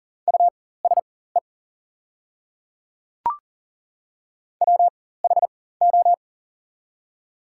Morse code, heard as a keyed tone, about 40 wpm, spaced as for 19 wpm, 700 Hz. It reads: USE WHO
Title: 4x Character Spacing - Top 100 Words - 40wpm